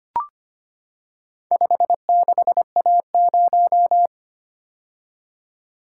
Morse code, heard as a keyed tone, about 25 wpm, 700 Hz. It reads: 56A0
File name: practice / 25wpm / Four Letter-Number Combinations - 25wpm